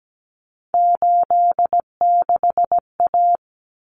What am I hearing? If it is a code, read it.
Morse code, heard as a keyed tone, about 17 wpm, 690 Hz. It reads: 86A
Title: Common Cut Numbers - 3-Digit 17wpm